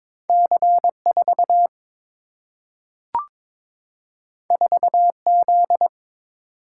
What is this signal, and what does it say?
Morse code, heard as a keyed tone, about 21 wpm, 700 Hz. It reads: C 4 4 Z
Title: Single-Letter Single-Number 22wpm